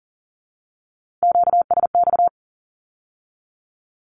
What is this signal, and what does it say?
Morse code, heard as a keyed tone, about 40 wpm, 700 Hz. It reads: QSX
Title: Q-Codes with Spoken Meaning - 40wpm